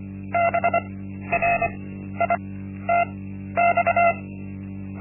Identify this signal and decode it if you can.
Morse code, heard as a keyed tone, about 24 wpm, spaced as for 12 wpm, 690 Hz. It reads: BRITX
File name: Linksmakalnis-1